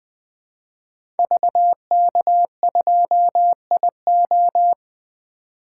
Morse code, heard as a keyed tone, about 20 wpm, 700 Hz. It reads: VK2IO
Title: 2023 POTA Activator Call Signs - 20wpm